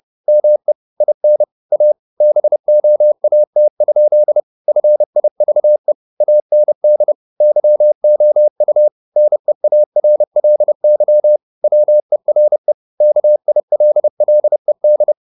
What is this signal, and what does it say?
Morse code, heard as a keyed tone, about 30 wpm, 600 Hz. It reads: ME IN A BOAT? FIVE AND YOU NEARLY WERE KILLED
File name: the old man and the sea / the old man and the sea 1 30wpm